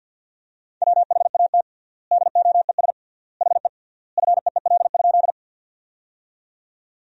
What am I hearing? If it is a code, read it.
Morse code, heard as a keyed tone, about 49 wpm, 700 Hz. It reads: WHAT DOES HE FEEL?
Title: 3x Word-Spacing - Sentences from Top 400 Words - 50wpm